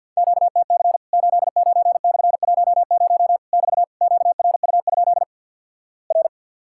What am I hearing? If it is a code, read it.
Morse code, heard as a keyed tone, about 50 wpm, 700 Hz. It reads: QTC 89/10 <BT> QRU? R
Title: Worked All Europe DX Contest - QTC Exchange - 50wpm